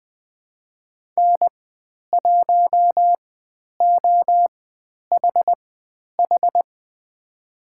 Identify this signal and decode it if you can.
Morse code, heard as a keyed tone, about 20 wpm, spaced as for 10 wpm, 700 Hz. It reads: N1OHH
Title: 102 Lesson 17g - Letter u - Call Signs 20wpm